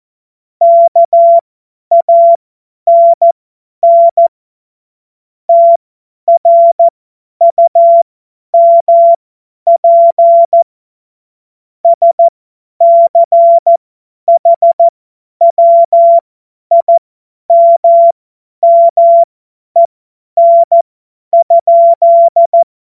Morse code, heard as a keyed tone, about 14 wpm, spaced as for 10 wpm, 680 Hz. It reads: KANN TRUMP SCHWIMMEN?
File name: Morsekurs / Witze / Witz 16. November 2021, 50 F70